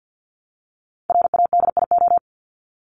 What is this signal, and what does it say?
Morse code, heard as a keyed tone, about 50 wpm, 700 Hz. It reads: RADIO
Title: Top 501-750 Words 50wpm